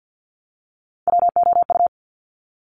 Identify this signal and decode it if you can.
Morse code, heard as a keyed tone, about 50 wpm, 700 Hz. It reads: WOU
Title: Top Three Letter Combinations in Words 50wpm